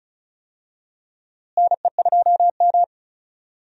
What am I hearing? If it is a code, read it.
Morse code, heard as a keyed tone, about 36 wpm, 700 Hz. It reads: NE2M
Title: USA Call Signs - 4 Characters - Repeated 2x - 35wpm